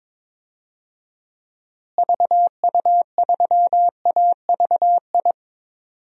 Morse code, heard as a keyed tone, about 22 wpm, 700 Hz. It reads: VU3AVI
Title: DXCC-100 Call Signs - 6 Characters - Repeated 2x - 22wpm